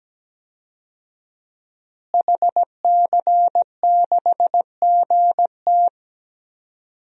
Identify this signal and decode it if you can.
Morse code, heard as a keyed tone, about 17 wpm, 700 Hz. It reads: HC6GT
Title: DXCC-100 Call Signs - 5 Characters - Repeated 3x - 17wpm